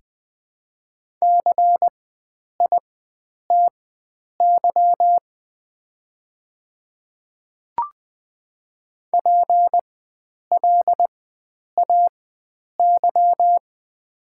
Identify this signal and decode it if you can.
Morse code, heard as a keyed tone, about 20 wpm, spaced as for 9 wpm, 700 Hz. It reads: CITY PLAY
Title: 4x Character Spacing - Top 300 Words - 20wpm